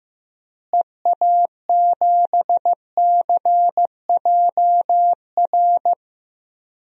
Morse code, heard as a keyed tone, about 15 wpm, 700 Hz. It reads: EA7CJR
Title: DXCC-100 Call Signs - 6 Characters - Repeated 3x - 15wpm